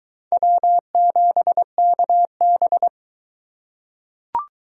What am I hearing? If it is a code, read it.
Morse code, heard as a keyed tone, about 20 wpm, 700 Hz. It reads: W7KB